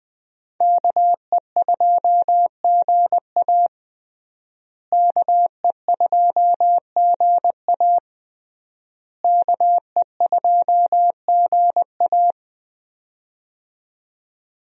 Morse code, heard as a keyed tone, about 21 wpm, 700 Hz. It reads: KE2GA KE2GA KE2GA
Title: USA Call Signs - 5 Characters - Repeated 3x - 20wpm